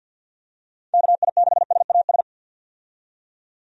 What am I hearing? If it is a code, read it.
Morse code, heard as a keyed tone, about 51 wpm, 700 Hz. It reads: KI6SAS
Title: USA Call Signs - 6 Characters - 50wpm